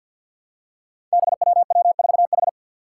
Morse code, heard as a keyed tone, about 52 wpm, 700 Hz. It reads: BWW4H
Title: Five Challenging Letter-Number Combinations 50wpm